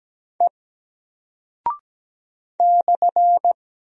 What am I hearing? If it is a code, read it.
Morse code, heard as a keyed tone, about 17 wpm, 700 Hz. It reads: E /
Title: Single Character (Letter:Number:Punctuation:Prosign) Alt 17wpm